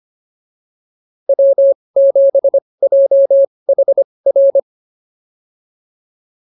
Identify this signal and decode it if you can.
Morse code, heard as a keyed tone, about 25 wpm, spaced as for 20 wpm, 590 Hz. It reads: W7JHR